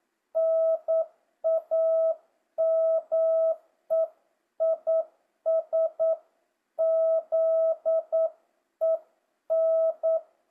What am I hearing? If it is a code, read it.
Morse code, heard as a keyed tone, about 9 wpm, 650 Hz. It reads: NAME IS ZEN
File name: jh8cbh / 2025cw